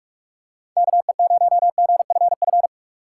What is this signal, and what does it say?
Morse code, heard as a keyed tone, about 43 wpm, 700 Hz. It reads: KE0GFF
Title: USA Call Signs - 6 Characters - 2x Character Spacing - 45wpm